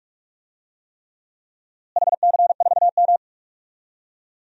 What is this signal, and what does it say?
Morse code, heard as a keyed tone, about 47 wpm, 700 Hz. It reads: HC4M
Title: DXCC-100 Call Signs - 4 Characters - Repeated 3x - 45wpm